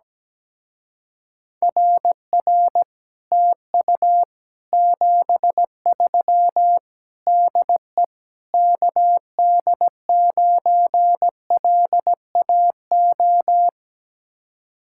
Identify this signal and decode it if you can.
Morse code, heard as a keyed tone, about 17 wpm, 700 Hz. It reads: RR TU 73 DE KD9LAO
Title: Speed-Racing Common QSO Phrases - 17wpm